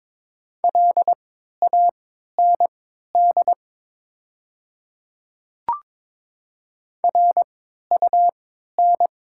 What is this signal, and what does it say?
Morse code, heard as a keyed tone, about 22 wpm, spaced as for 13 wpm, 700 Hz. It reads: LAND RUN
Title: 3x Character Spacing - Top 500 Words - 22wpm